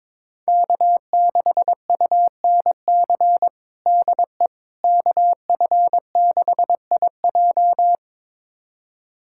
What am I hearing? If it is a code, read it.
Morse code, heard as a keyed tone, about 22 wpm, 700 Hz. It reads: K6UNC DE KF6IJ
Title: USA Callsign DE Callsign - 5 Characters - 22wpm